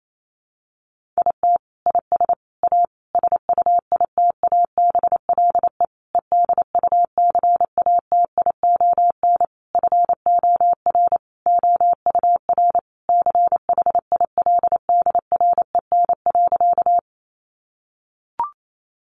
Morse code, heard as a keyed tone, about 28 wpm, 700 Hz. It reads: IT IS A SUITABLE EDUCATION FOR OUR CHILDREN.
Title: Sentences from Top 1400 Words 28wpm